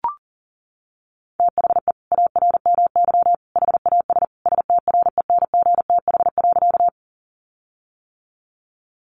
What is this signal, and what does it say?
Morse code, heard as a keyed tone, about 40 wpm, 700 Hz. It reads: THE ARMY HAS STRENGTH.
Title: Sentences from Top 1300 Words 40wpm